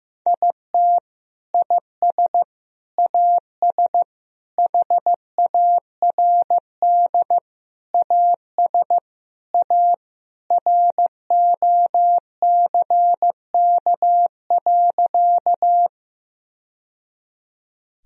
Morse code, heard as a keyed tone, about 15 wpm, 700 Hz. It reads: IT IS AS HARD AS A ROCK.
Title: Sentences from Top 1000 Words - Review Part 2 - 15wpm